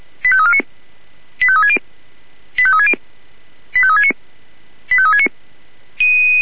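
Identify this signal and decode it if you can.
Five-tone selective call (ZVEI), heard as quick runs of five tones, tones of about 70 ms. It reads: 85349 85360 85360 85369 85369
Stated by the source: Received with an SDR; ZVEI five-tone selective call.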